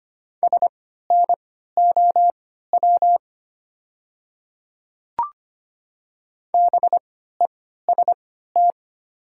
Morse code, heard as a keyed tone, about 25 wpm, spaced as for 14 wpm, 700 Hz. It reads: SNOW BEST
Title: 3x Character Spacing - Top 500 Words - 25wpm